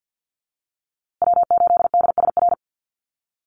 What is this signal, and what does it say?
Morse code, heard as a keyed tone, about 50 wpm, 700 Hz. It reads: W8DSR
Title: USA Call Signs Repeated 3x 50wpm